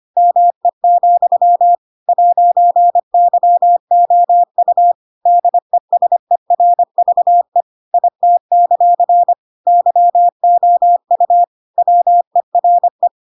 Morse code, heard as a keyed tone, about 25 wpm, 700 Hz. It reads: ME, 'YOU DESERVE IT; YOU WERE
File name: Pinocchio / Pinocchio ch7 25 25 ik1hge